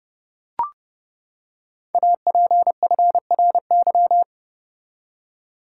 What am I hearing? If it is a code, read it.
Morse code, heard as a keyed tone, about 30 wpm, 700 Hz. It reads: APFRY
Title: Code Groups - 5 Letters - 30wpm